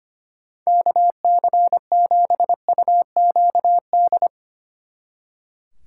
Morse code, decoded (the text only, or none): KC7UQD